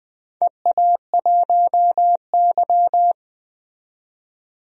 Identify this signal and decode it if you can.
Morse code, heard as a keyed tone, about 20 wpm, 700 Hz. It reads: EA1Y
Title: DXCC-100 Call Signs - 4 Characters - 20wpm